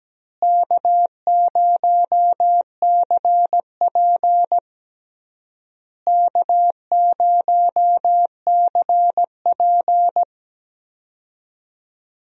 Morse code, heard as a keyed tone, about 17 wpm, 700 Hz. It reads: K0CP K0CP
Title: USA Call Signs - 4 Characters - Repeated 3x - 17wpm